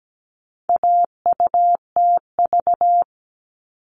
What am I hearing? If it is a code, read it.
Morse code, heard as a keyed tone, about 17 wpm, 700 Hz. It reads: AUTV
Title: All Cut Numbers - 4-Digit 17wpm